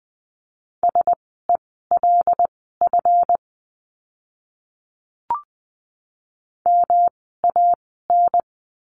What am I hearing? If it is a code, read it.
Morse code, heard as a keyed tone, about 20 wpm, spaced as for 14 wpm, 700 Hz. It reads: SELF MAN
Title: 139 Lesson 23f - Letter f - Words 20wpm